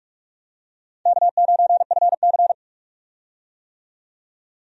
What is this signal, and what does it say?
Morse code, heard as a keyed tone, about 46 wpm, 700 Hz. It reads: K9FC
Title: USA Call Signs - 4 Characters - 45wpm